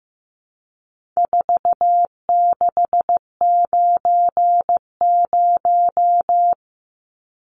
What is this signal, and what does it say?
Morse code, heard as a keyed tone, about 15 wpm, 700 Hz. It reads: 4690
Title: Numbers - 4-Digit 15wpm